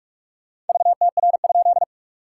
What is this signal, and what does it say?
Morse code, heard as a keyed tone, about 44 wpm, 690 Hz. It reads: VTR?
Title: SOTA QSO Practice 45wpm